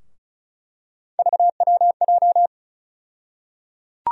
Morse code, heard as a keyed tone, about 36 wpm, 700 Hz. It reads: VWJ
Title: USA Call Sign Postfixes - 3 Letters 35wpm